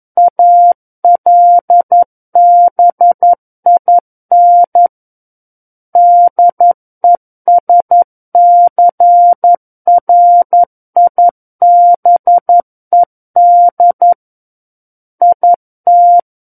Morse code, heard as a keyed tone, about 11 wpm, 700 Hz. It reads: ALBIN DESCRIBED IT
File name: 11 WPM